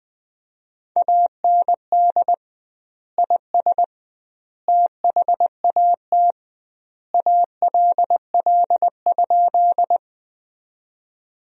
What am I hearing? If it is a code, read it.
Morse code, heard as a keyed tone, about 20 wpm, 700 Hz. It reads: AND IS THAT ALL?
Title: 2x Word-Spacing - Sentences from Top 100 Words - 20wpm